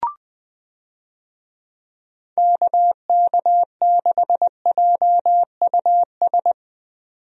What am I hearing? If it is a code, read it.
Morse code, heard as a keyed tone, about 20 wpm, 700 Hz. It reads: KK6JUS
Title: USA Call Signs - 6 Characters - 20wpm